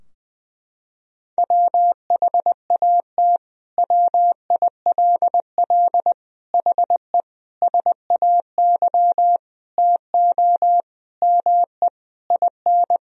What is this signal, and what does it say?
Morse code, heard as a keyed tone, about 20 wpm, 700 Hz. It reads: WHAT WILL HE SAY TO ME IN
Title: Sentences from Top 1000 Words - Review Part 3 - 20wpm